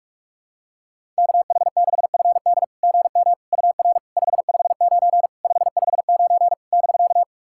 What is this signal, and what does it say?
Morse code, heard as a keyed tone, about 44 wpm, 690 Hz. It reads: KH6FD GM UR 559 559 <BK>
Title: SOTA QSO Practice 45wpm